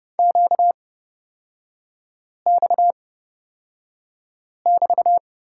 Morse code, heard as a keyed tone, about 31 wpm, spaced as for 5 wpm, 690 Hz. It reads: QX<BT>
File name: course / 30:10wpm / 236 Lesson 38e - Letter q - Three Characters 30wpm